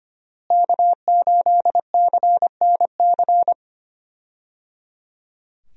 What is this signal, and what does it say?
Morse code, heard as a keyed tone, about 25 wpm, 700 Hz. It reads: K8CNC
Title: USA Call Signs - 5 Characters - Repeated 3x - 25wpm